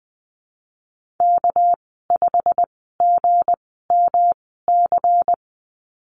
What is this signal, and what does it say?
Morse code, heard as a keyed tone, about 20 wpm, spaced as for 14 wpm, 700 Hz. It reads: K5GMC
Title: 245 Lesson 39g - Letter z - Call Signs 20wpm